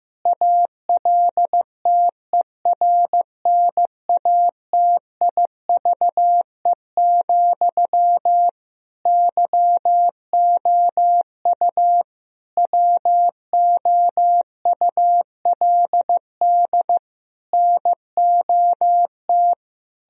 Morse code, heard as a keyed tone, about 15 wpm, 700 Hz. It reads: ALTERNATIVE, YOU WOULD NOT
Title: Sentences from Top 1000 Words - Review Part 4 - 15wpm